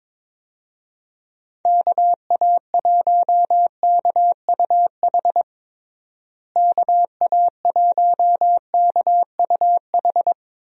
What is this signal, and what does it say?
Morse code, heard as a keyed tone, about 21 wpm, 700 Hz. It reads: KA1KUH KA1KUH
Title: USA Call Signs - 6 Characters - Repeated 3x - 22wpm